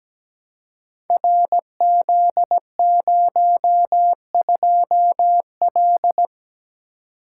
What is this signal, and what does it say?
Morse code, heard as a keyed tone, about 17 wpm, 700 Hz. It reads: RZ02L
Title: Code Groups - 5 Letter:Numbers - 17wpm